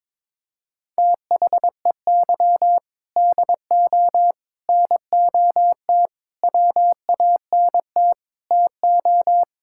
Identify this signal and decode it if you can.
Morse code, heard as a keyed tone, about 22 wpm, 700 Hz. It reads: THEY DO NOT WANT TO
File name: practice / 22wpm / Sentences with 4-Word Phrases - 22wpm